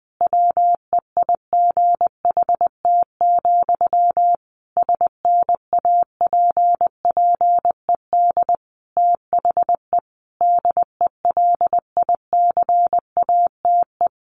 Morse code, heard as a keyed tone, about 19 wpm, 700 Hz. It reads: WEIGHT, SNAPPED THE DELICATE
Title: A Sound Of Thunder 20wpm